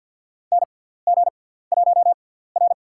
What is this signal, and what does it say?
Morse code, heard as a keyed tone, about 52 wpm, spaced as for 17 wpm, 700 Hz. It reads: NG1R
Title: USA Call Signs - 4 Characters - 6x Character Spacing - 50wpm